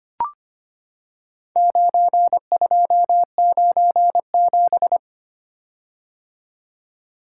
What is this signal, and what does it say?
Morse code, heard as a keyed tone, about 25 wpm, 700 Hz. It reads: 9297
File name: practice / 25wpm / Four Letter-Number Combinations - 25wpm